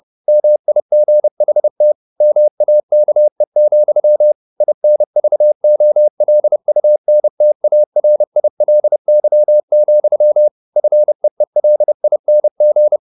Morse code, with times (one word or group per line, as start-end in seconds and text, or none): 0.3-1.9 MIGHT
2.2-4.3 MAKE,
4.6-10.5 INVOLUNTARILY,
10.8-13.0 FEELING